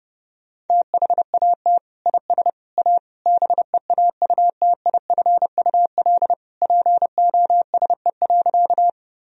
Morse code, heard as a keyed tone, about 32 wpm, 700 Hz. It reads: THAT IS A BEAUTIFUL POSE.
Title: Sentences from Top 1000 Words - Review Part 2 - 30wpm